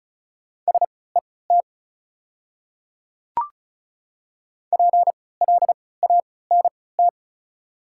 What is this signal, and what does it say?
Morse code, heard as a keyed tone, about 34 wpm, spaced as for 20 wpm, 700 Hz. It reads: SET PLANT
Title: 3x Character Spacing - Top 500 Words - 35wpm